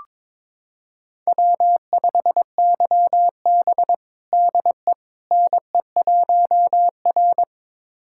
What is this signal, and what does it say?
Morse code, heard as a keyed tone, about 22 wpm, 700 Hz. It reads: W5YB DE NE1R
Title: USA Callsign DE Callsign - 4 Characters - 22wpm